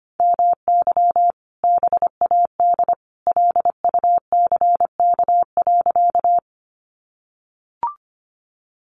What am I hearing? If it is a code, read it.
Morse code, heard as a keyed tone, about 25 wpm, 700 Hz. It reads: MY BAD LUCK.
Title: Sentences from Top 1500 Words 25wpm